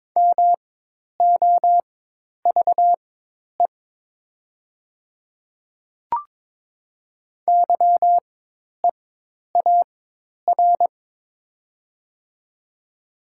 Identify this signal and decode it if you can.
Morse code, heard as a keyed tone, about 22 wpm, spaced as for 10 wpm, 700 Hz. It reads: MOVE YEAR
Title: 4x Character Spacing - Top 200 Words - 22wpm